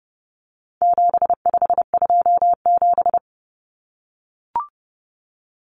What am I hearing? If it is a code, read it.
Morse code, heard as a keyed tone, about 30 wpm, 700 Hz. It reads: 7527
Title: Numbers - 4-Digit 30wpm